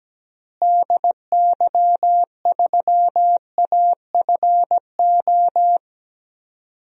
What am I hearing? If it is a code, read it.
Morse code, heard as a keyed tone, about 17 wpm, 700 Hz. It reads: DY3AFO
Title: DXCC-100 Call Signs - 6 Characters - Repeated 2x - 17wpm